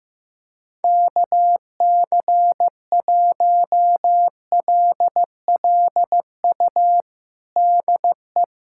Morse code, heard as a keyed tone, about 15 wpm, 700 Hz. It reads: KC1LLU DE
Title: USA Callsign DE Callsign 15wpm